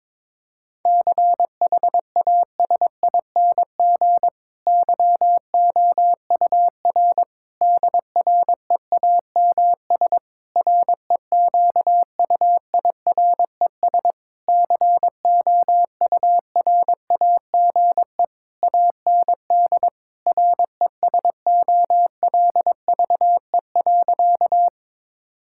Morse code, heard as a keyed tone, about 22 wpm, 700 Hz. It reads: CHASING YOUR DREAMS REQUIRES COURAGE AND RESOLVE.
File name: Difficult Sentences with Hints - 22wpm